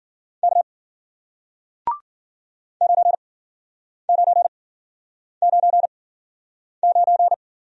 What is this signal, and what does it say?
Morse code, heard as a keyed tone, about 53 wpm, spaced as for 9 wpm, 700 Hz. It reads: K 9999